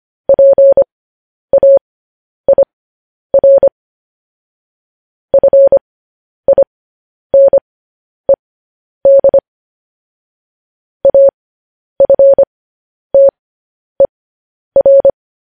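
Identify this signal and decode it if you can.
Morse code, heard as a keyed tone, about 25 wpm, spaced as for 10 wpm, 550 Hz. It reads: PAIR FINED AFTER
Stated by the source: Morse code news headlines sent as a reading transcription, preceded by vvv.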